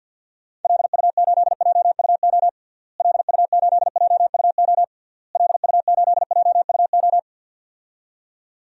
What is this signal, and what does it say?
Morse code, heard as a keyed tone, about 51 wpm, 700 Hz. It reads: LU8JUO LU8JUO LU8JUO
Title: DXCC-100 Call Signs - 6 Characters - Repeated 3x - 50wpm